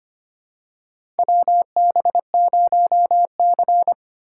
Morse code, heard as a keyed tone, about 25 wpm, 700 Hz. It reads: WB0C
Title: USA Call Signs - 4 Characters - 3x Character Spacing - 25wpm